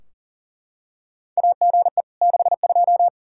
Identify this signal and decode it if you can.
Morse code, heard as a keyed tone, about 40 wpm, 700 Hz. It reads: AGE 62
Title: Practice QSO - Exchange 3 - More Info - 40wpm